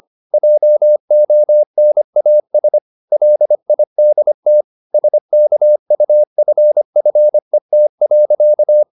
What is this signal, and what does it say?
Morse code, heard as a keyed tone, about 25 wpm, 600 Hz. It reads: JONAS LIDT SKUFFET.
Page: Ferie 25 wpm